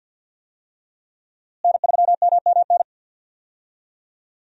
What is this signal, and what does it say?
Morse code, heard as a keyed tone, about 49 wpm, 700 Hz. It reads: N3MMN
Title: USA Call Signs - 5 Characters - Repeated 3x - 50wpm